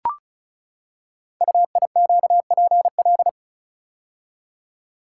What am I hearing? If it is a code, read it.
Morse code, heard as a keyed tone, about 36 wpm, 700 Hz. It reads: UIQPL
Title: Code Groups - 5 Letters - 35wpm